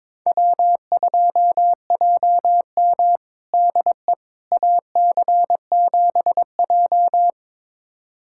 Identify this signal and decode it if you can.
Morse code, heard as a keyed tone, about 22 wpm, 700 Hz. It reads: W2JM DE AC7J
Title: USA Callsign DE Callsign 22wpm